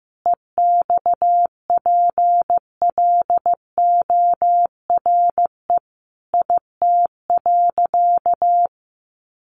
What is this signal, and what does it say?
Morse code, heard as a keyed tone, about 15 wpm, 700 Hz. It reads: EXPLORE IT.
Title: Sentences from Top 1100 Words 15wpm